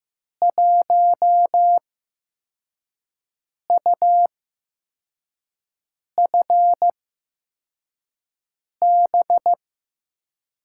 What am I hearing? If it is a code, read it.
Morse code, heard as a keyed tone, about 15 wpm, spaced as for 4 wpm, 700 Hz. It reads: 1UFB
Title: USA Call Signs - 5 Characters - 8x Character Spacing - 15wpm